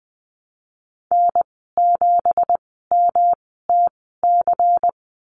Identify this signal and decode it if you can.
Morse code, heard as a keyed tone, about 20 wpm, spaced as for 15 wpm, 700 Hz. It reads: N7MTC
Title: 167 Lesson 27g - Number 7 - Call Signs 20wpm